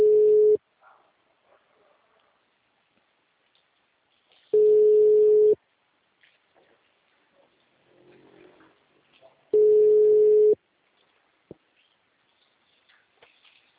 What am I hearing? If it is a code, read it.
Ringback tone (Europe, 425 Hz).
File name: out- -2013- - - .87850